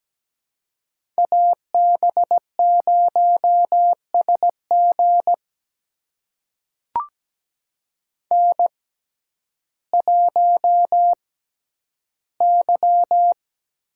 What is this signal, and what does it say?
Morse code, heard as a keyed tone, about 17 wpm, spaced as for 10 wpm, 700 Hz. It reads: AB0SG N1Y
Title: USA Call Signs - 5 Characters - 6x Character Spacing - 17wpm